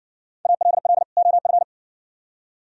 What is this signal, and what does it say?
Morse code, heard as a keyed tone, about 60 wpm, 700 Hz. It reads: ALL OF